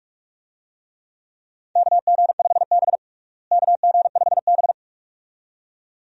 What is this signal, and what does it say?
Morse code, heard as a keyed tone, about 44 wpm, 700 Hz. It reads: KG5B KG5B
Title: USA Call Signs - 4 Characters - Repeated 2x - 45wpm